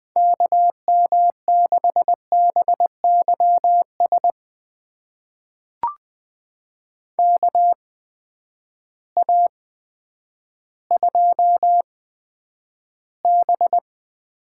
Morse code, heard as a keyed tone, about 20 wpm, spaced as for 10 wpm, 700 Hz. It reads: KM6BYS KA2B
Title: USA Call Signs - 6 Characters - 8x Character Spacing - 20wpm